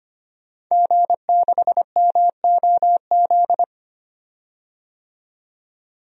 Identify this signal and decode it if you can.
Morse code, heard as a keyed tone, about 25 wpm, 700 Hz. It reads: G6MOZ
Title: DXCC-100 Call Signs - 5 Characters - 25wpm